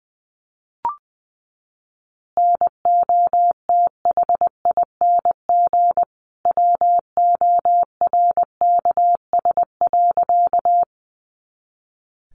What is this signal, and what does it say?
Morse code, heard as a keyed tone, about 20 wpm, 700 Hz. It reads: NOTHING WORKS.
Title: Additional Easy Phrases 20wpm